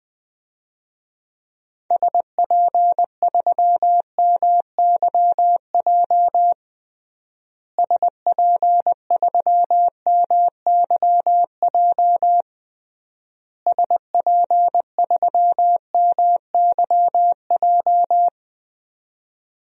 Morse code, heard as a keyed tone, about 21 wpm, 700 Hz. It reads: SP3MYJ SP3MYJ SP3MYJ
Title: DXCC-100 Call Signs - 6 Characters - Repeated 3x - 20wpm